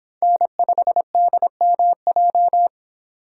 Morse code, heard as a keyed tone, about 30 wpm, 700 Hz. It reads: N5DMJ